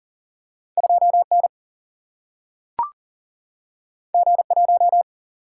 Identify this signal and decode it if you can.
Morse code, heard as a keyed tone, about 40 wpm, 700 Hz. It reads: 2N G1
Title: Single-Letter Single-Number 40wpm